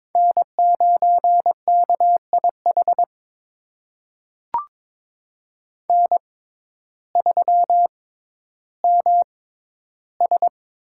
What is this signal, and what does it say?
Morse code, heard as a keyed tone, about 22 wpm, spaced as for 12 wpm, 700 Hz. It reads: N9KIH N3MS
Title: USA Call Signs - 5 Characters - 6x Character Spacing - 22wpm